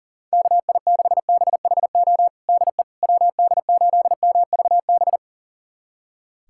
Morse code, heard as a keyed tone, about 40 wpm, 700 Hz. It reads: KI6BHO DE WD8MVB
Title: USA Callsign DE Callsign 40wpm